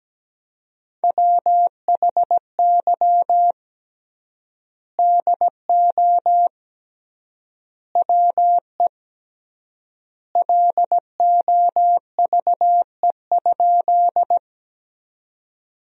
Morse code, heard as a keyed tone, about 17 wpm, 700 Hz. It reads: WHY DO WE LOVE?
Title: 3x Word-Spacing - Sentences from Top 400 Words - 17wpm